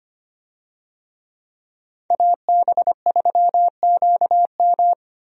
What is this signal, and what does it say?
Morse code, heard as a keyed tone, about 25 wpm, 700 Hz. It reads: AB3QM
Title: USA Call Signs - 5 Characters - 25wpm